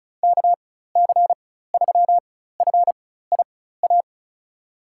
Morse code, heard as a keyed tone, about 36 wpm, spaced as for 16 wpm, 690 Hz. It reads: KC3FIA
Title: USA Call Signs - 4x Character Spacing - 35wpm